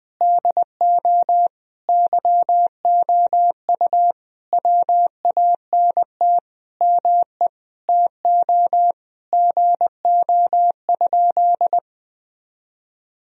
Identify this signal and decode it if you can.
Morse code, heard as a keyed tone, about 21 wpm, 700 Hz. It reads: DO YOU WANT ME TO GO?
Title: Sentences from Top 1000 Words - Review Part 3 - 20wpm